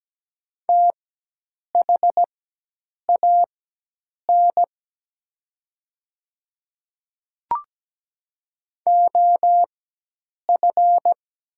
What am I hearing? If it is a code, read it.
Morse code, heard as a keyed tone, about 17 wpm, spaced as for 8 wpm, 700 Hz. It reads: THAN OF
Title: 4x Character Spacing - Top 100 Words - 17wpm